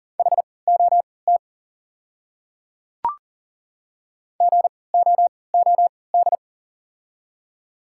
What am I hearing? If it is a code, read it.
Morse code, heard as a keyed tone, about 40 wpm, spaced as for 23 wpm, 690 Hz. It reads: HOT GOOD
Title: 3x Character Spacing - Top 500 Words - 40wpm